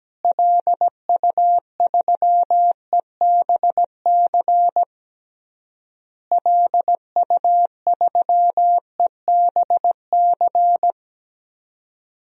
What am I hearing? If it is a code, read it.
Morse code, heard as a keyed tone, about 17 wpm, 700 Hz. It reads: LU3EBC LU3EBC
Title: DXCC-100 Call Signs - 6 Characters - Repeated 3x - 17wpm